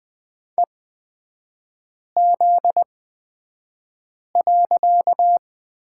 Morse code, heard as a keyed tone, about 20 wpm, spaced as for 5 wpm, 700 Hz. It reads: EZ.
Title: 255 Lesson 41e - Period - Three Characters 20wpm